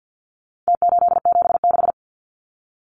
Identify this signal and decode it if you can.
Morse code, heard as a keyed tone, about 50 wpm, 700 Hz. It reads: T876
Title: Numbers - Contest Style - 4-Digit 50wpm